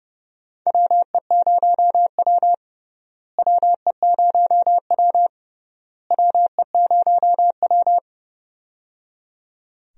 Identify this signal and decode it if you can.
Morse code, heard as a keyed tone, about 29 wpm, 700 Hz. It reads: WE0W WE0W WE0W
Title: USA Call Signs - 4 Characters - Repeated 3x - 30wpm